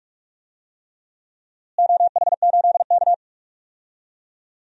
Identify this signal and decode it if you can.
Morse code, heard as a keyed tone, about 46 wpm, 700 Hz. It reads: OH8K
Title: DXCC-100 Call Signs - 4 Characters - Repeated 3x - 45wpm